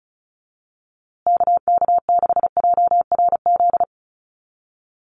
Morse code, heard as a keyed tone, about 34 wpm, 700 Hz. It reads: KK6JRZ
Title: USA Call Signs Repeated 2x 35wpm